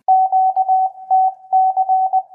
Morse code, heard as a keyed tone, about 20 wpm, 760 Hz. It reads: QTC